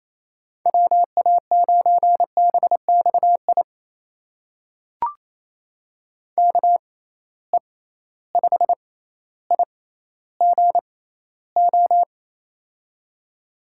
Morse code, heard as a keyed tone, about 28 wpm, spaced as for 14 wpm, 700 Hz. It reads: WA9BXI KE5IGO
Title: USA Call Signs - 6 Characters - 6x Character Spacing - 28wpm